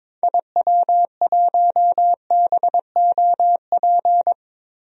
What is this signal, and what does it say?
Morse code, heard as a keyed tone, about 22 wpm, 700 Hz. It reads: IW1BOP
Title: DXCC-100 Call Signs - 6 Characters - Repeated 2x - 22wpm